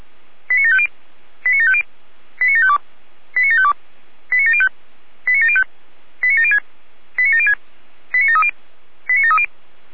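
Five-tone selective call (ZVEI), heard as quick runs of five tones, tones of about 70 ms. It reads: 89750 89750 89752 89752 89705 89705 89706 89706 89730 89730
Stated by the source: ZVEI five-tone selective call received by SDR.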